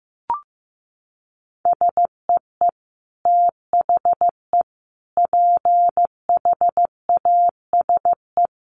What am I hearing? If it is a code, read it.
Morse code, heard as a keyed tone, about 15 wpm, 700 Hz. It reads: SEE THE PHASE
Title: Sentences from Top 1200 Words 15wpm